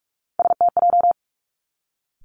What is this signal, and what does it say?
Morse code, heard as a keyed tone, about 46 wpm, 700 Hz. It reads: STJ
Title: Three Challenging Letter-Number Combinations 45wpm